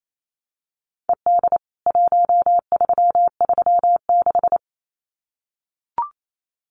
Morse code, heard as a keyed tone, about 28 wpm, 700 Hz. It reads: ED 1336
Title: CWT Exchange 28wpm